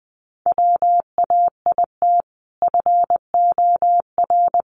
Morse code, heard as a keyed tone, about 20 wpm, 700 Hz. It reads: WAIT FOR
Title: Sentences from Top 500 Words 20wpm